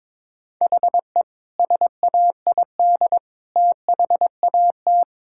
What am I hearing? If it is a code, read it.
Morse code, heard as a keyed tone, about 22 wpm, 700 Hz. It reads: HE SAID THAT
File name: practice / 22wpm / Sentences with 8-Word Phrases - 22wpm